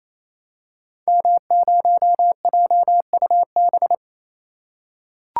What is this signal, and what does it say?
Morse code, heard as a keyed tone, about 27 wpm, 700 Hz. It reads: M0JUB
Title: Code Groups - 5 Letter:Numbers - 28wpm